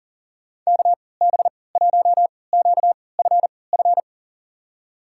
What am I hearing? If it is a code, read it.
Morse code, heard as a keyed tone, about 40 wpm, spaced as for 23 wpm, 700 Hz. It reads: KB1QFF
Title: USA Call Signs - 6 Characters - 3x Character Spacing - 40wpm